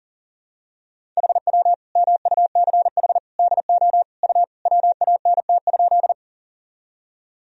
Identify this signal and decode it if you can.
Morse code, heard as a keyed tone, about 40 wpm, 700 Hz. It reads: HW MUCH DO U WANT?
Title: Sentences from Top 200 Words and Abbreviations 40wpm